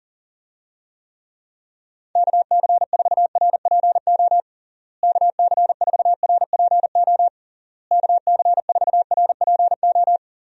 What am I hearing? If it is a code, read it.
Morse code, heard as a keyed tone, about 40 wpm, 700 Hz. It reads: KC4RPO KC4RPO KC4RPO
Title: USA Call Signs - 6 Characters - Repeated 3x - 40wpm